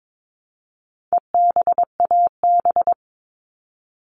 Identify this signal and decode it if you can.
Morse code, heard as a keyed tone, about 22 wpm, 700 Hz. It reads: EBAB
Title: All Cut Numbers - 4-Digit 22wpm